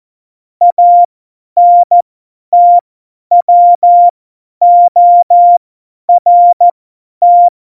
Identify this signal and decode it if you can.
Morse code, heard as a keyed tone, about 14 wpm, spaced as for 10 wpm, 700 Hz. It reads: ANTWORT
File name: Morsekurs / Witze / Witz 02. Oktober 2018, 50 F70